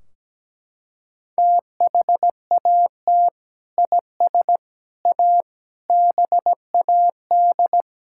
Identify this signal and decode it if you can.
Morse code, heard as a keyed tone, about 17 wpm, 700 Hz. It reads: THAT IS A BAD
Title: Sentences from Top 1600 Words 17wpm